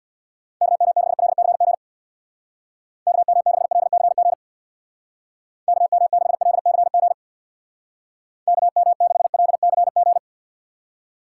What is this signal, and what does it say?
Morse code, heard as a keyed tone, about 67 wpm, 700 Hz. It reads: KM6LCG KM6LCG KM6LCG KM6LCG